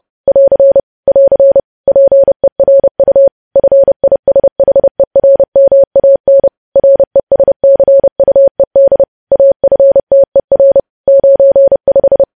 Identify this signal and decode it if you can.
Morse code, heard as a keyed tone, about 30 wpm, 550 Hz. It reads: <AR> <AR> PERU FISHERMAN RESCUED AFTER 95
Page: News Headlines in Morse Code at 30 WPM